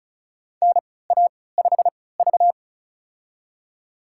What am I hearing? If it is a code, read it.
Morse code, heard as a keyed tone, about 36 wpm, spaced as for 20 wpm, 700 Hz. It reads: NA5V
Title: USA Call Signs - 4 Characters - 3x Character Spacing - 35wpm